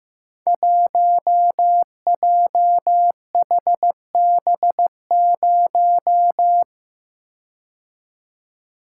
Morse code, heard as a keyed tone, about 15 wpm, 700 Hz. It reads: 1JHB0
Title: Code Groups - 5 Letter:Numbers - 15wpm